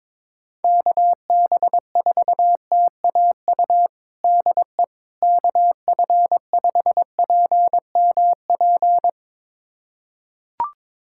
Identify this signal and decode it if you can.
Morse code, heard as a keyed tone, about 22 wpm, 700 Hz. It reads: KB4TAU DE KF5PMP
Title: USA Callsign DE Callsign - 6 Characters - 22wpm